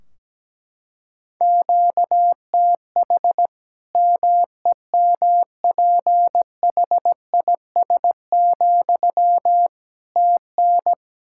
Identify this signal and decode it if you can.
Morse code, heard as a keyed tone, about 17 wpm, 700 Hz. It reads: QTH MEMPHIS, TN
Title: Practice QSO - Exchange 1 - Beginning - 17wpm